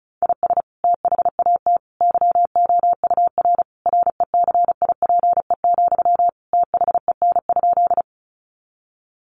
Morse code, heard as a keyed tone, about 34 wpm, 700 Hz. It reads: IS THAT YOUR RECIPE, THEN?
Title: Sentences from Top 1200 Words 35wpm